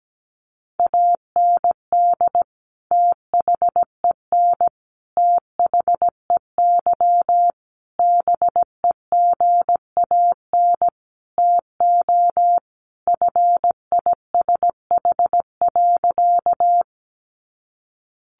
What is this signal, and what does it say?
Morse code, heard as a keyed tone, about 17 wpm, 700 Hz. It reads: AND THEN THEY BEGAN TO FISH.
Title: Sentences from Top 300 Words 17wpm